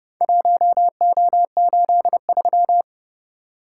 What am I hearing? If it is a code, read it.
Morse code, heard as a keyed tone, about 27 wpm, 700 Hz. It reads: 1O83